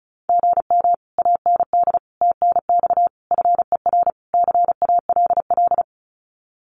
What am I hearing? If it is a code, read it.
Morse code, heard as a keyed tone, about 34 wpm, 700 Hz. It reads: GM AND TNX FER CALL
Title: Common QSO Phrases Encore 35wpm